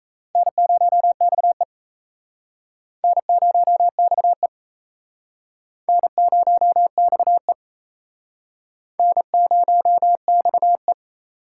Morse code, heard as a keyed tone, about 36 wpm, 700 Hz. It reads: N0XE N0XE N0XE N0XE